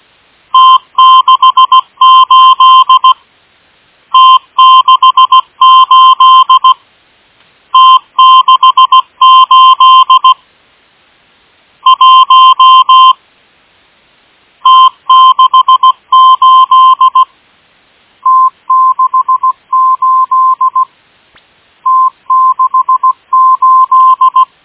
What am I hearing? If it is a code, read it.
Morse code, heard as a keyed tone, about 16 wpm, 1040 Hz. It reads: T68 T68 T68 1 T68 T68 T68
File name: M12- -1830utc-8157khz